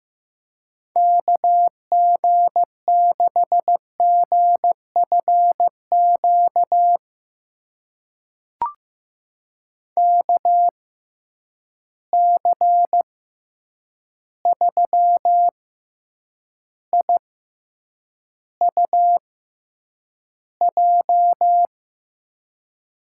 Morse code, heard as a keyed tone, about 15 wpm, spaced as for 8 wpm, 700 Hz. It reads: KG6GFQ KC3IUJ
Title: USA Call Signs - 6 Characters - 6x Character Spacing - 15wpm